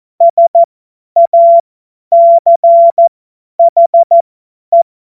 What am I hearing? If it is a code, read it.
Morse code, heard as a keyed tone, about 14 wpm, spaced as for 10 wpm, 680 Hz. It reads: SACHE
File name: Morsekurs / Witze / Witz 02. August 2022 50 F70